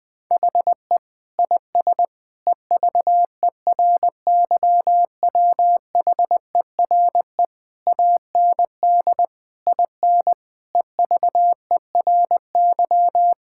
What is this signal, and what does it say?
Morse code, heard as a keyed tone, about 20 wpm, 700 Hz. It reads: HE IS EVERYWHERE AND IN EVERY
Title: Sentences from Top 1600 Words 20wpm